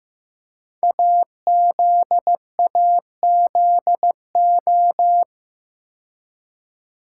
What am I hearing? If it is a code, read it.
Morse code, heard as a keyed tone, about 15 wpm, 700 Hz. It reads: AZAZO
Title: Code Groups - 5 Letters - 15wpm